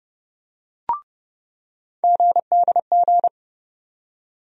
Morse code, heard as a keyed tone, about 32 wpm, 700 Hz. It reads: GDG
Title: USA Call Sign Postfixes - 3 Letters 30wpm